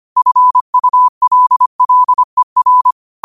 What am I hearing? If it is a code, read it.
Morse code, heard as a keyed tone, about 25 wpm, 1000 Hz. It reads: FULLER
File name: qotdfiles / 25wpm